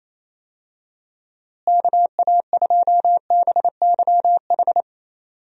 Morse code, heard as a keyed tone, about 28 wpm, 700 Hz. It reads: KA2BYH
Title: USA Call Signs - 6 Characters - Repeated 2x - 28wpm